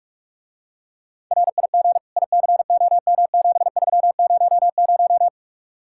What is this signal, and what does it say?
Morse code, heard as a keyed tone, about 44 wpm, 700 Hz. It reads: RIG ICOM7300